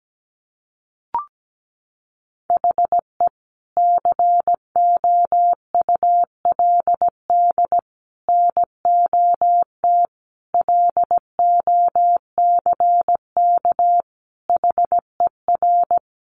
Morse code, heard as a keyed tone, about 17 wpm, 700 Hz. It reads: HE COULD NOT LOCK HER
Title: Sentences from Top 1300 Words 17wpm